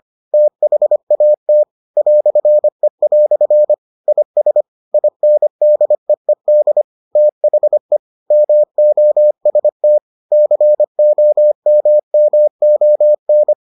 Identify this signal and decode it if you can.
Morse code, heard as a keyed tone, about 25 wpm, 600 Hz. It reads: THAT "E" IS INDEED THE MOST COMMON